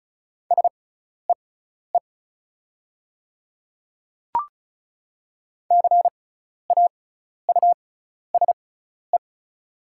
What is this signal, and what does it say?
Morse code, heard as a keyed tone, about 34 wpm, spaced as for 12 wpm, 700 Hz. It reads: SEE CAUSE
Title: 6x Character Spacing - Top 200 Words - 35wpm